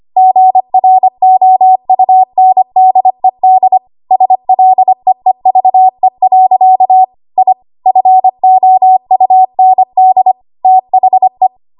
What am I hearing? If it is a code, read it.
Morse code, heard as a keyed tone, about 25 wpm, 750 Hz. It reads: GROUNDED SLEEVE. I FOUND THE
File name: Archive / 25 WPM / 25WPM